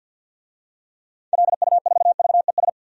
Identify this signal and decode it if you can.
Morse code, heard as a keyed tone, about 51 wpm, 700 Hz. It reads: LU4VES
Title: DXCC-100 Call Signs - 6 Characters - 50wpm